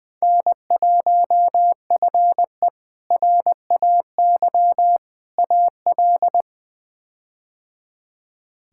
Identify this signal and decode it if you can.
Morse code, heard as a keyed tone, about 21 wpm, 700 Hz. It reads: N1FE RAY AL
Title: K1USN SST QSO Practice - 20wpm